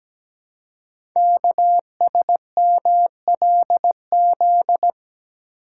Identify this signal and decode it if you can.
Morse code, heard as a keyed tone, about 17 wpm, 700 Hz. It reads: KSMLZ
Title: Code Groups - 5 Letters - 17wpm